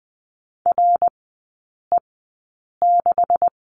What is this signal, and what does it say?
Morse code, heard as a keyed tone, about 20 wpm, spaced as for 8 wpm, 700 Hz. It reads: RE6
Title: 172 Lesson 28e - Number 9 - Three Characters 20wpm